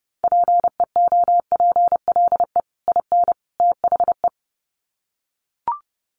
Morse code, heard as a keyed tone, about 30 wpm, 700 Hz. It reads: PEOPLE IN THE
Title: Sets of 3 Words 30wpm